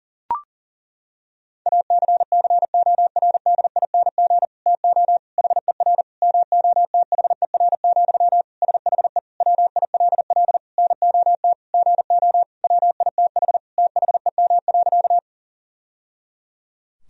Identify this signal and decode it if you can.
Morse code, heard as a keyed tone, about 40 wpm, 700 Hz. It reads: ACCORDING TO HER MOTHER, SHE WILL NOT GO WITH THEM.
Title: Sentences from Top 1000 Words - Review Part 4 - 40wpm